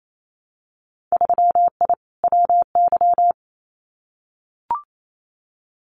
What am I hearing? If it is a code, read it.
Morse code, heard as a keyed tone, about 28 wpm, 700 Hz. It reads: 3I WY
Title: Winter Field Day Exchanges Encore 28wpm